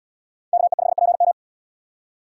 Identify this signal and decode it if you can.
Morse code, heard as a keyed tone, about 77 wpm, 700 Hz. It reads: KE5PM